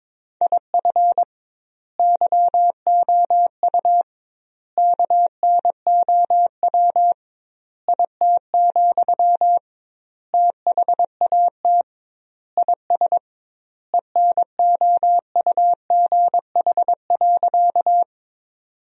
Morse code, heard as a keyed tone, about 22 wpm, 700 Hz. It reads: IF YOU KNOW IT, THAT IS ENOUGH.
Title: 2x Word-Spacing - Sentences from Top 400 Words - 22wpm